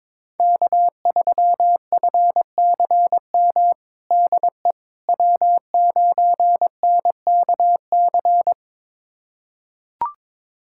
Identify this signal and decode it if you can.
Morse code, heard as a keyed tone, about 21 wpm, 700 Hz. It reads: K3FCM DE W9NKC
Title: USA Callsign DE Callsign - 5 Characters - 22wpm